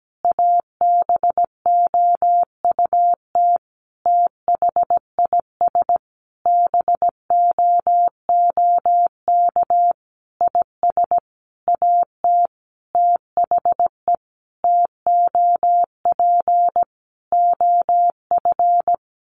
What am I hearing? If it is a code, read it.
Morse code, heard as a keyed tone, about 17 wpm, 700 Hz. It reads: ABOUT THIS BOOK IS AT THE TOP OF
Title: Sentences from Top 1200 Words 17wpm